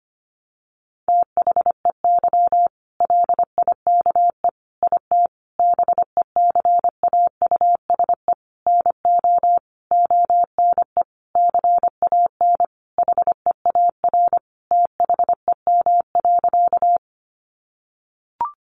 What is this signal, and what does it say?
Morse code, heard as a keyed tone, about 25 wpm, 700 Hz. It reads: THEY LIKE IT BECAUSE NO ONE CAN HEAR THEM.
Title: Sentences from Top 600 Words 25wpm